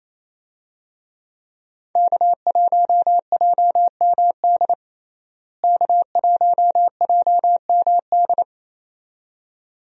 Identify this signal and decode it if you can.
Morse code, heard as a keyed tone, about 27 wpm, 700 Hz. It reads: K1JMD K1JMD
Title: USA Call Signs - 5 Characters - Repeated 2x - 28wpm